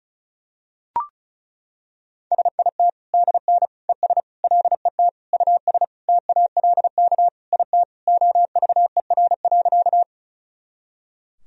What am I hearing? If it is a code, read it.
Morse code, heard as a keyed tone, about 36 wpm, 700 Hz. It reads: SIT DN ES LET US TALK IT OVER.
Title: Sentences from Top 500 Words and Abbreviations 35wpm